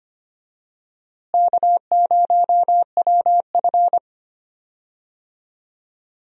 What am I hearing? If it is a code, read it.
Morse code, heard as a keyed tone, about 25 wpm, 700 Hz. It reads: K0WF
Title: USA Call Signs - 4 Characters - Repeated 3x - 25wpm